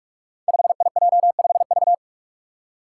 Morse code, heard as a keyed tone, about 44 wpm, 700 Hz. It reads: 5IJ5V
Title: Five Challenging Letter-Number Combinations 45wpm